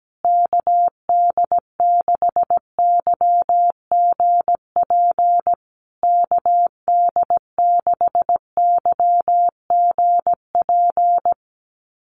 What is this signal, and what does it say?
Morse code, heard as a keyed tone, about 17 wpm, 700 Hz. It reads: KD6YGP KD6YGP
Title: USA Call Signs Repeated 3x 17wpm